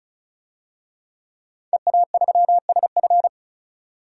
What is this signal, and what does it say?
Morse code, heard as a keyed tone, about 36 wpm, 700 Hz. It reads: EA3SF
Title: DXCC-100 Call Signs - 5 Characters - 35wpm